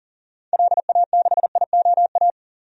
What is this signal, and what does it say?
Morse code, heard as a keyed tone, about 40 wpm, 700 Hz. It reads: LA6IOA
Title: DXCC-100 Call Signs - 6 Characters - 40wpm